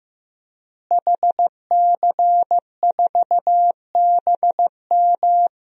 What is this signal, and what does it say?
Morse code, heard as a keyed tone, about 15 wpm, 700 Hz. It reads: HC4BM
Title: DXCC-100 Call Signs - 5 Characters - Repeated 3x - 15wpm